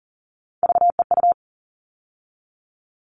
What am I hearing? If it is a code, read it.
Morse code, heard as a keyed tone, about 40 wpm, 700 Hz. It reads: VEU
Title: All Cut Numbers - 3-Digit 40wpm